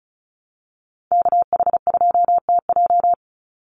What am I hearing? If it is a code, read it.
Morse code, heard as a keyed tone, about 36 wpm, 700 Hz. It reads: KH2TJ
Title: NA SOTA Chaser Callsigns 35wpm